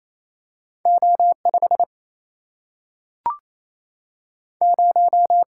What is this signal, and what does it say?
Morse code, heard as a keyed tone, about 28 wpm, 700 Hz. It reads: O5 0
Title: Single-Letter Single-Number 28wpm